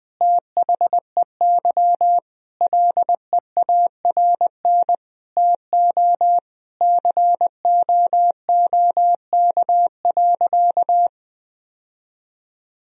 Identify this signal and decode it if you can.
Morse code, heard as a keyed tone, about 20 wpm, 700 Hz. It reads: THEY LEARN TO COOK.
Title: Sentences from Top 1000 Words - Review Part 4 - 20wpm